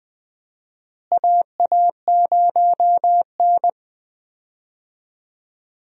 Morse code, heard as a keyed tone, about 21 wpm, 700 Hz. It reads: AA0N
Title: USA Call Signs - 4 Characters - Repeated 2x - 20wpm